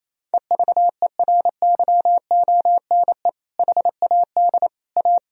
Morse code, heard as a keyed tone, about 28 wpm, 700 Hz. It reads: EVERYONE HAD A
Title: Sentences from Top 1000 Words - Review Part 2 - 28wpm